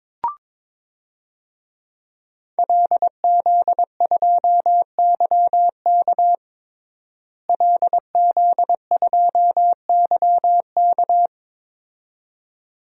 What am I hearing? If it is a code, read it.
Morse code, heard as a keyed tone, about 22 wpm, 700 Hz. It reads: LZ2YK LZ2YK
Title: DXCC-100 Call Signs - 5 Characters - Repeated 2x - 22wpm